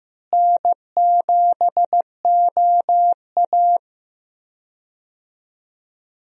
Morse code, heard as a keyed tone, about 16 wpm, 700 Hz. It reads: N7OA
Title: USA Call Signs - 4 Characters - Repeated 3x - 15wpm